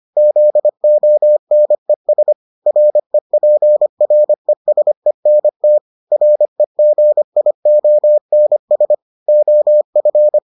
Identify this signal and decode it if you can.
Morse code, heard as a keyed tone, about 25 wpm, 600 Hz. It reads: ZONES REPRESENT REGIONS OF